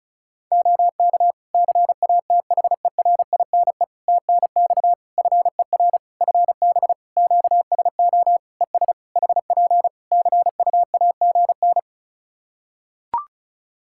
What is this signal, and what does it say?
Morse code, heard as a keyed tone, about 36 wpm, 700 Hz. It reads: OK CATHERINE TNX FER FB QSO ES HP CUAGN
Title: Practice QSO - Exchange 4 - Ending - 35wpm